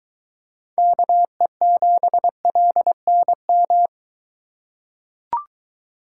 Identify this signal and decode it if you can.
Morse code, heard as a keyed tone, about 23 wpm, 700 Hz. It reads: KE7LNM